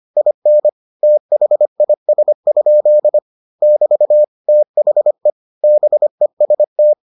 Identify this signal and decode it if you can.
Morse code, heard as a keyed tone, about 25 wpm, 600 Hz. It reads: IN THIS? <BT> THE BEST